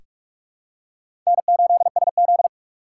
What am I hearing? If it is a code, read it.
Morse code, heard as a keyed tone, about 47 wpm, 700 Hz. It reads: N8SZ
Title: USA Call Signs - 4 Characters - 2x Character Spacing - 45wpm